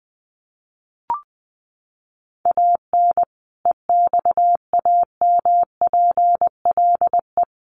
Morse code, heard as a keyed tone, about 20 wpm, 700 Hz. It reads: AN EXAMPLE
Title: Sentences from Top 400 Words 20wpm